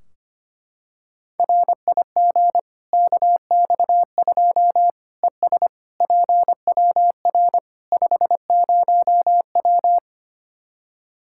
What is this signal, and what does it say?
Morse code, heard as a keyed tone, about 25 wpm, 700 Hz. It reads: RIG KX2 ES PWR 50W
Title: Practice QSO - Exchange 2 - Info - 25wpm